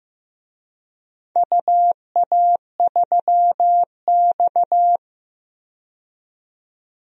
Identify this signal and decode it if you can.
Morse code, heard as a keyed tone, about 15 wpm, 700 Hz. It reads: UA3X
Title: DXCC-100 Call Signs - 4 Characters - 15wpm